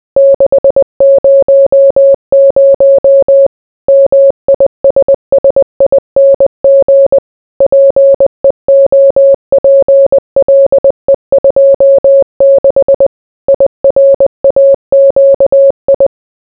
Morse code, heard as a keyed tone, about 20 wpm, 550 Hz. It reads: 600 MISSING PEOPLE 26 IRAQI